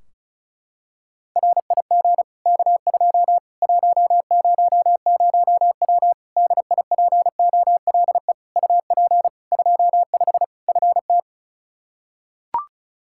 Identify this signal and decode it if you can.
Morse code, heard as a keyed tone, about 36 wpm, 700 Hz. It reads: RIG K2 100W DIPOLE UP 25 FT
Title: Practice QSO - Exchange 2 - Info - 35wpm